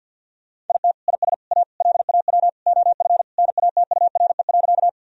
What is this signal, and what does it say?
Morse code, heard as a keyed tone, about 46 wpm, 700 Hz. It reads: IT IS A LAW OF NATURE.
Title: Sentences from Top 1000 Words - Review Part 3 - 50wpm